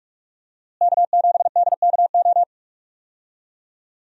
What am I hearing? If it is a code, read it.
Morse code, heard as a keyed tone, about 46 wpm, 700 Hz. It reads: K7DKO
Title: USA Call Signs - 5 Characters - Repeated 2x - 45wpm